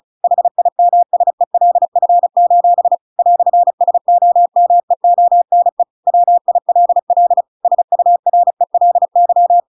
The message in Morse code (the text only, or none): HIMSELF: "SOMEONE WILL SURELY